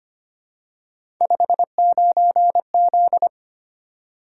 Morse code, heard as a keyed tone, about 25 wpm, 700 Hz. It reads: 59Z
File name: Three Letter-Number Combinations - 25wpm